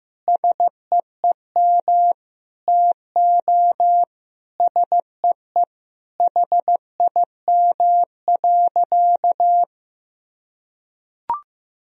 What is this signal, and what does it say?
Morse code, heard as a keyed tone, about 15 wpm, 700 Hz. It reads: SEEM TO SEE HIM.
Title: Sentences from Top 1000 Words - Review Part 1 - 15wpm